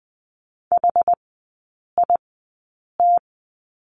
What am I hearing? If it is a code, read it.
Morse code, heard as a keyed tone, about 20 wpm, spaced as for 8 wpm, 700 Hz. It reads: HIT
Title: 067 Lesson 12e - Letter d - Three Characters 20wpm